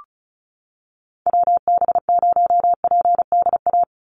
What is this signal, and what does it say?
Morse code, heard as a keyed tone, about 34 wpm, 700 Hz. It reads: WB0PDA
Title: USA Call Signs 35wpm